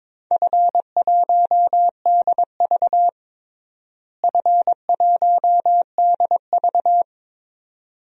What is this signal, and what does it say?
Morse code, heard as a keyed tone, about 22 wpm, 700 Hz. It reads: F1DV F1DV
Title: DXCC-100 Call Signs - 4 Characters - Repeated 2x - 22wpm